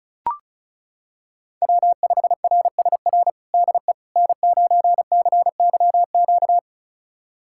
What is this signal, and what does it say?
Morse code, heard as a keyed tone, about 36 wpm, 700 Hz. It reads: W5RSR DE N9CYQ
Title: USA Callsign DE Callsign - 5 Characters - 35wpm